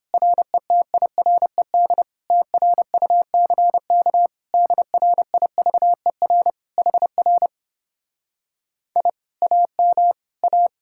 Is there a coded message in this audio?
Yes: Morse code from 0.1 to 10.7 s, heard as a keyed tone, about 31 wpm, 700 Hz. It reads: RETIRED TRUCK DRIVER HR I AM A